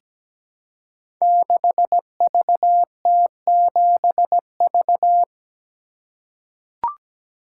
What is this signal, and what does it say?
Morse code, heard as a keyed tone, about 17 wpm, 700 Hz. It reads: 6VT7V
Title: Code Groups - 5 Letter:Numbers - 17wpm